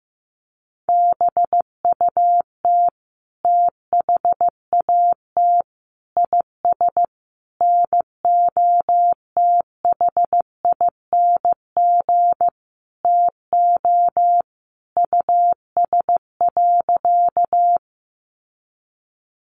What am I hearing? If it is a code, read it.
Morse code, heard as a keyed tone, about 15 wpm, 700 Hz. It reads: BUT THAT IS NOTHING TO US.
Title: Sentences from Top 500 Words 15wpm